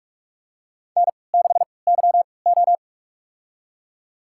Morse code, heard as a keyed tone, about 41 wpm, spaced as for 25 wpm, 700 Hz. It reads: N6YO
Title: USA Call Signs - 4 Characters - 3x Character Spacing - 45wpm